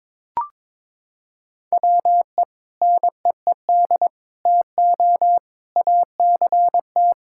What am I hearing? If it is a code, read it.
Morse code, heard as a keyed tone, about 22 wpm, 700 Hz. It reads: WE NEED TO ACT
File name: practice / 22wpm / Sentences with 2-Word Phrases - 22wpm